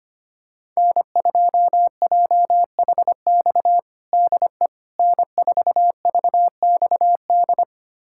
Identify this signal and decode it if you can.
Morse code, heard as a keyed tone, about 25 wpm, 700 Hz. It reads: N2JHX DE N4VXD
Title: USA Callsign DE Callsign - 5 Characters - 25wpm